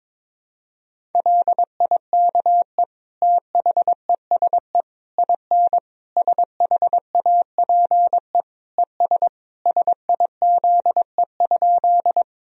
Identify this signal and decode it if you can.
Morse code, heard as a keyed tone, about 22 wpm, 700 Hz. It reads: LIKE THESE IN SHAPE ES SIZE?
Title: Sentences from Top 500 Words and Abbreviations 22wpm